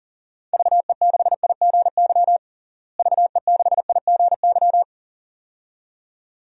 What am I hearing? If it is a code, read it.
Morse code, heard as a keyed tone, about 40 wpm, 700 Hz. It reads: VE6IGY VE6IGY
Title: DXCC-100 Call Signs - 6 Characters - Repeated 2x - 40wpm